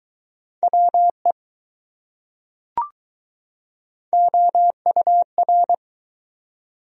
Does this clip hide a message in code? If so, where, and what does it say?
Yes: Morse code from 0.6 to 5.7 s, heard as a keyed tone, about 21 wpm, 700 Hz. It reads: WE OUR